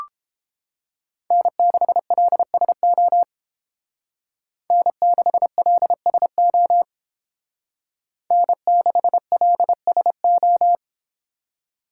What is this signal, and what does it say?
Morse code, heard as a keyed tone, about 28 wpm, 700 Hz. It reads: N6LSO N6LSO N6LSO